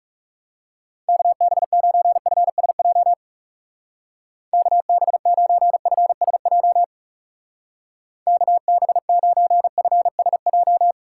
Morse code, heard as a keyed tone, about 37 wpm, 700 Hz. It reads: KB9FSJ KB9FSJ KB9FSJ
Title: Speed Racing - USA 2x3 Call Signs 30wpm